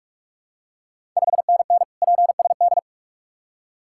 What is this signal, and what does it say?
Morse code, heard as a keyed tone, about 46 wpm, 690 Hz. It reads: 5NN PSD
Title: Ontario QSO Party Exchange 45wpm